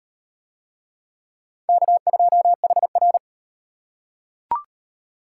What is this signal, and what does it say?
Morse code, heard as a keyed tone, about 38 wpm, 700 Hz. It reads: K2HR